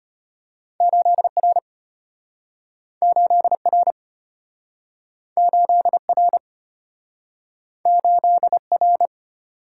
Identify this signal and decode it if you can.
Morse code, heard as a keyed tone, about 34 wpm, 700 Hz. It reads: 8R 8R 8R 8R